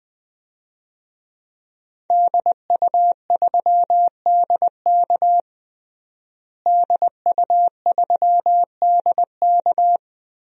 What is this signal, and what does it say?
Morse code, heard as a keyed tone, about 20 wpm, 700 Hz. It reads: DU3DK DU3DK
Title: DXCC-100 Call Signs - 5 Characters - Repeated 3x - 20wpm